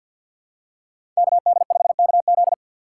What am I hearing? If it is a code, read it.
Morse code, heard as a keyed tone, about 52 wpm, 700 Hz. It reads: KD5KZ
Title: USA Call Signs - 6x Character Spacing - 50wpm